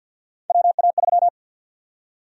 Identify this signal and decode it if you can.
Morse code, heard as a keyed tone, about 52 wpm, 700 Hz. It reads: WA3
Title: USA Call Sign Prefixes - 50wpm